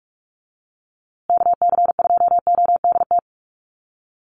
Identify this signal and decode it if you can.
Morse code, heard as a keyed tone, about 48 wpm, 700 Hz. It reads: KC2ODT
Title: USA Call Signs Repeated 2x 45wpm